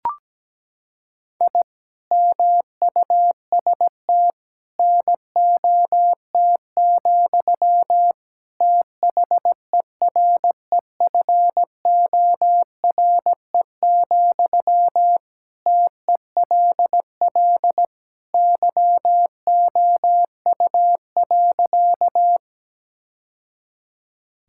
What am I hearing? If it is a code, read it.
Morse code, heard as a keyed tone, about 17 wpm, 700 Hz. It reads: I MUST NOT, THEREFORE, TELL YOU.
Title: Sentences from Top 1000 Words - Review Part 5 - 17wpm